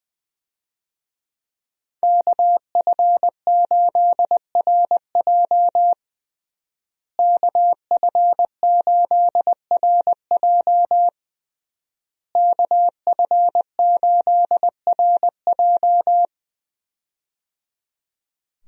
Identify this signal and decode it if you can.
Morse code, heard as a keyed tone, about 20 wpm, 700 Hz. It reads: KF8RJ KF8RJ KF8RJ
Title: USA Call Signs - 5 Characters - Repeated 3x - 20wpm